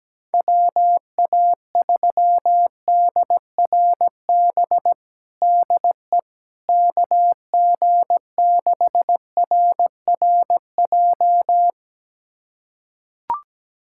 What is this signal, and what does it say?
Morse code, heard as a keyed tone, about 17 wpm, 700 Hz. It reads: WA3DRB DE KG6RRJ
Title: USA Callsign DE Callsign - 6 Characters - 17wpm